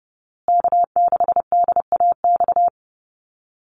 Morse code, heard as a keyed tone, about 30 wpm, 700 Hz. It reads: K6DAX
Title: USA Call Signs Repeated 3x 30wpm